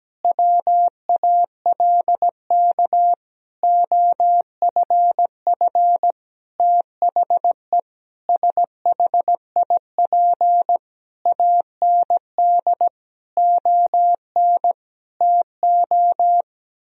Morse code, heard as a keyed tone, about 17 wpm, 700 Hz. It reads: WALK OFF THE SHIP AND ON TO
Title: Sentences from Top 1000 Words - Review Part 2 - 17wpm